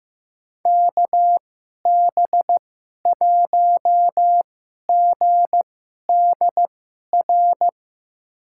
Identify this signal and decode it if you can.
Morse code, heard as a keyed tone, about 15 wpm, spaced as for 11 wpm, 700 Hz. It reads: KB1GDR
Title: USA Call Signs - 6 Characters - 2x Character Spacing - 15wpm